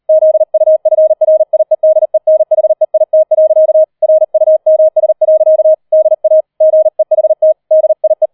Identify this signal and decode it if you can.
Morse code, heard as a keyed tone, about 38 wpm, 630 Hz. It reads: ZUFRIEDENHEIT. RUMS. DA GEHT DIE
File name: 004 Vierter Streich 40wpm